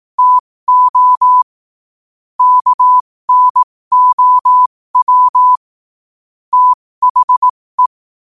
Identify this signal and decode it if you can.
Morse code, heard as a keyed tone, about 18 wpm, spaced as for 15 wpm, 1000 Hz. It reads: TO KNOW THE
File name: qotdfiles / 15wpm